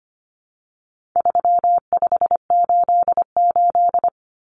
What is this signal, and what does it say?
Morse code, heard as a keyed tone, about 25 wpm, 700 Hz. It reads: 3588
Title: Common Cut Numbers - 4-Digit 25wpm